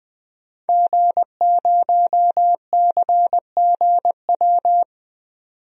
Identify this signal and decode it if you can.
Morse code, heard as a keyed tone, about 20 wpm, 700 Hz. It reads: G0CGW
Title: DXCC-100 Call Signs - 5 Characters - 20wpm